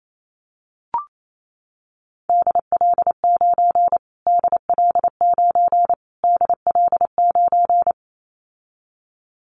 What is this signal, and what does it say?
Morse code, heard as a keyed tone, about 28 wpm, 700 Hz. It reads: DL9 DL9 DL9
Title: Callsign Prefixes for Top 25 DXCC - Repeated 3x - 28wpm